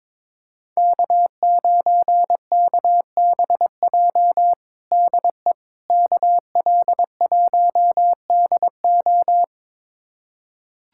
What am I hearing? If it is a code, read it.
Morse code, heard as a keyed tone, about 22 wpm, 700 Hz. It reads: K9KBJ DE KL1DO
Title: USA Callsign DE Callsign - 5 Characters - 22wpm